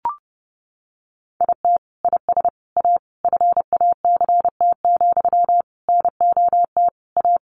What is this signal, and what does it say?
Morse code, heard as a keyed tone, about 30 wpm, 700 Hz. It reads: IT IS A FACT, NOT A
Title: Sentences from Top 1400 Words 30wpm